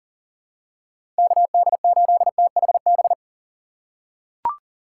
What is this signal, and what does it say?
Morse code, heard as a keyed tone, about 40 wpm, 700 Hz. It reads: KD8THB
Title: USA Call Signs - 6 Characters - 4x Character Spacing - 40wpm